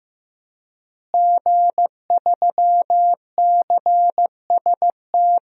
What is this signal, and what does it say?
Morse code, heard as a keyed tone, about 16 wpm, 700 Hz. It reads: G3CST
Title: DXCC-100 Call Signs - 5 Characters - Repeated 3x - 15wpm